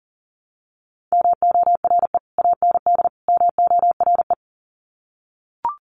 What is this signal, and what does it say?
Morse code, heard as a keyed tone, about 40 wpm, 700 Hz. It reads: MORE AND MORE
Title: Sets of 3 Words 40wpm